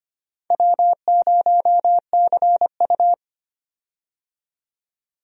Morse code, heard as a keyed tone, about 25 wpm, 700 Hz. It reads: W0CU
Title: USA Call Signs - 4 Characters - 25wpm